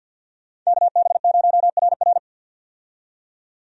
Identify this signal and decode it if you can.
Morse code, heard as a keyed tone, about 51 wpm, 700 Hz. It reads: KB0RR
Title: USA Call Signs - 5 Characters - 50wpm